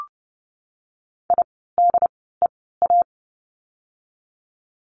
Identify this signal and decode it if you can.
Morse code, heard as a keyed tone, about 30 wpm, spaced as for 17 wpm, 700 Hz. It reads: IDEA
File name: course / 30:10wpm / 108 Lesson 18f - Letter m - Words 30wpm